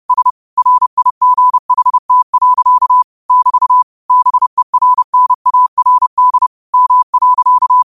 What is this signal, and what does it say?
Morse code, heard as a keyed tone, about 30 wpm, 1000 Hz. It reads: S RIGHT. <BT> BERNARD M.